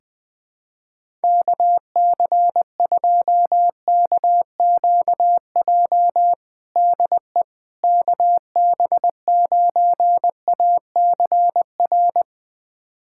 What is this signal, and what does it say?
Morse code, heard as a keyed tone, about 21 wpm, 700 Hz. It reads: KC2KQJ DE KB9ACR
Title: USA Callsign DE Callsign - 6 Characters - 20wpm